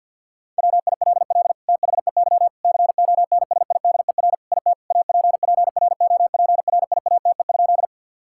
Morse code, heard as a keyed tone, about 51 wpm, 700 Hz. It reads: WILL THEY CONSIDER IT APPROPRIATE?
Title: Sentences from Top 1000 Words - Review Part 5 - 50wpm